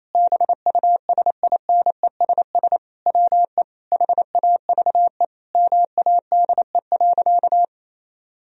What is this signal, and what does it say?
Morse code, heard as a keyed tone, about 28 wpm, 700 Hz. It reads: BUSINESS WE HAVE MADE.
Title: Sentences from Top 1000 Words - Review Part 2 - 28wpm